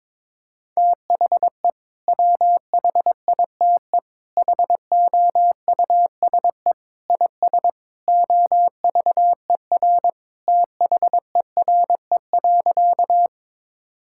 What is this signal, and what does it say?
Morse code, heard as a keyed tone, about 22 wpm, 700 Hz. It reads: THE WHITE HOUSE IS OVER THERE.
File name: practice / 22wpm / Sentences with 3-Word Phrases - 22wpm